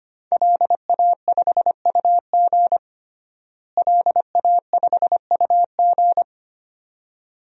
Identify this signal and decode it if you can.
Morse code, heard as a keyed tone, about 25 wpm, 700 Hz. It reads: LA5UG LA5UG
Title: DXCC-100 Call Signs - 5 Characters - Repeated 2x - 25wpm